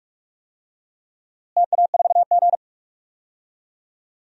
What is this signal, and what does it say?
Morse code, heard as a keyed tone, about 42 wpm, 700 Hz. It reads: TA4G
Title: DXCC-100 Call Signs - 4 Characters - Repeated 3x - 45wpm